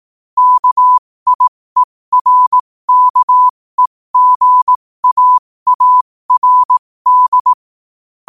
Morse code, heard as a keyed tone, about 18 wpm, spaced as for 15 wpm, 1000 Hz. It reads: KIERKEGAARD
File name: qotdfiles / 15wpm